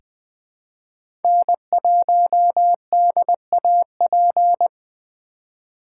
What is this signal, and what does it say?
Morse code, heard as a keyed tone, about 20 wpm, 700 Hz. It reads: N1DAP
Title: USA Call Signs - 5 Characters - 3x Character Spacing - 20wpm